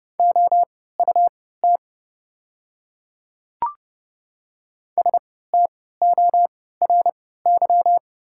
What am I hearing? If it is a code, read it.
Morse code, heard as a keyed tone, about 30 wpm, spaced as for 17 wpm, 700 Hz. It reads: OUT STORY
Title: 3x Character Spacing - Top 300 Words - 30wpm